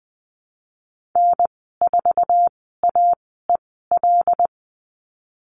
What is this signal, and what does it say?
Morse code, heard as a keyed tone, about 19 wpm, spaced as for 14 wpm, 700 Hz. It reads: N4AEL
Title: 128 Lesson 21g - Number 6 - Call Signs 20wpm